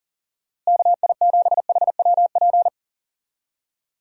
Morse code, heard as a keyed tone, about 37 wpm, 700 Hz. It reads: KI7HWP
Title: USA Call Signs - 6 Characters - Repeated 3x - 40wpm